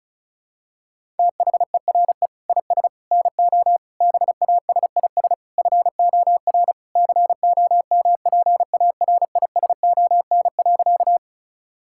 Morse code, heard as a keyed tone, about 34 wpm, 700 Hz. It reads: THERE IS NO BASIS FOR COMPARISON.
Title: Sentences from Top 1600 Words 35wpm